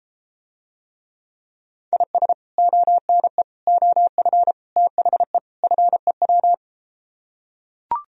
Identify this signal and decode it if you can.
Morse code, heard as a keyed tone, about 33 wpm, 700 Hz. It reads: IS ONE OF THE FEW